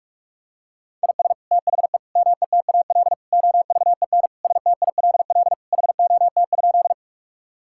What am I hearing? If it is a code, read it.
Morse code, heard as a keyed tone, about 41 wpm, 700 Hz. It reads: IS THE METAL OVEN STILL HOT?
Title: Sentences from Top 1000 Words - Review Part 6 - 45wpm